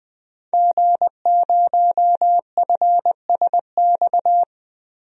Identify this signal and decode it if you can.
Morse code, heard as a keyed tone, about 21 wpm, 700 Hz. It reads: G0FSX
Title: DXCC-100 Call Signs - 5 Characters - Repeated 3x - 20wpm